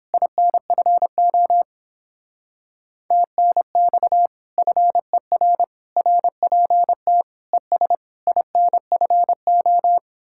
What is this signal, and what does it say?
Morse code, heard as a keyed tone, about 29 wpm, 700 Hz. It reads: INFO TNX FER RPT ES INFO